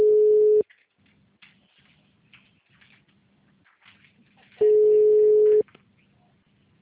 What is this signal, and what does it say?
Ringback tone (Europe, 425 Hz).
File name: out- -2014- - - .30